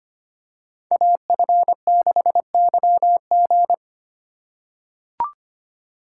Morse code, heard as a keyed tone, about 25 wpm, 700 Hz. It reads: AF6YG
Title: USA Call Signs - 5 Characters - 6x Character Spacing - 25wpm